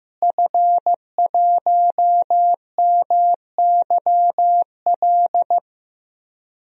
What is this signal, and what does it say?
Morse code, heard as a keyed tone, about 15 wpm, 700 Hz. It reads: F1MYL
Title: DXCC-100 Call Signs - 5 Characters - Repeated 3x - 15wpm